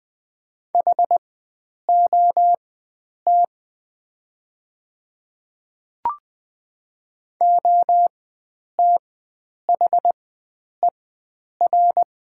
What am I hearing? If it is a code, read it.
Morse code, heard as a keyed tone, about 20 wpm, spaced as for 9 wpm, 700 Hz. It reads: HOT OTHER
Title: 4x Character Spacing - Top 100 Words - 20wpm